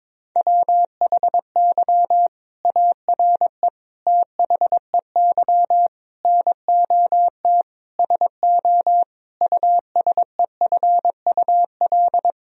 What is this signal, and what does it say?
Morse code, heard as a keyed tone, about 22 wpm, 700 Hz. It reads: WHY ARE THEY NOT SO USEFUL
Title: Sentences from Top 1000 Words - Review Part 4 - 22wpm